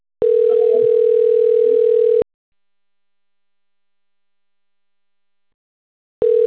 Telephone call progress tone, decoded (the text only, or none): ringback tone